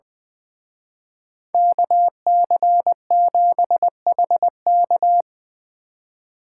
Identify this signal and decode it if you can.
Morse code, heard as a keyed tone, about 23 wpm, 700 Hz. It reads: KC7HK
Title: Speed Racing - USA 2x2 Call Signs 20wpm